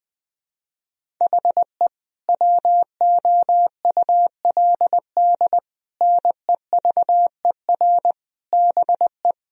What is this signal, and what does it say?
Morse code, heard as a keyed tone, about 20 wpm, 700 Hz. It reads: HE WOULD NEVER BE
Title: Sentences from Top 1000 Words - Review Part 3 - 20wpm